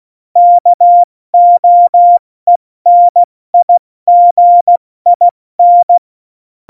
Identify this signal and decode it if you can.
Morse code, heard as a keyed tone, about 16 wpm, 700 Hz. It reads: KOENIGIN
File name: Morsekurs / Witze / Witz 01. Dezember 2020, 70 F80